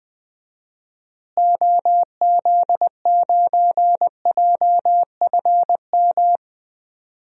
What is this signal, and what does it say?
Morse code, heard as a keyed tone, about 20 wpm, 700 Hz. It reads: OZ9JFM
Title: DXCC-100 Call Signs - 6 Characters - Repeated 2x - 20wpm